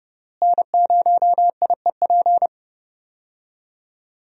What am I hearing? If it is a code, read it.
Morse code, heard as a keyed tone, about 30 wpm, 700 Hz. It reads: N0IEP
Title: USA Call Signs - 5 Characters - Repeated 3x - 30wpm